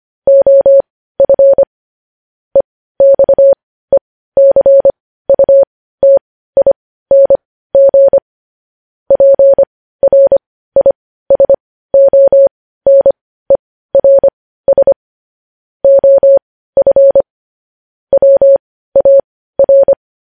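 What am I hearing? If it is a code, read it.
Morse code, heard as a keyed tone, about 25 wpm, spaced as for 15 wpm, 550 Hz. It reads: OF EXECUTING PRISONERS OF WAR
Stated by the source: Morse code news headlines sent at 15 wpm.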